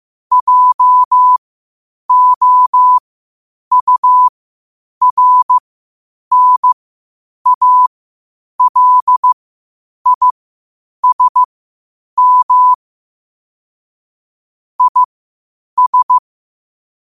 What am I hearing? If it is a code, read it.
Morse code, heard as a keyed tone, about 15 wpm, spaced as for 8 wpm, 1000 Hz. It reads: JOURNALISM IS
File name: 8wpm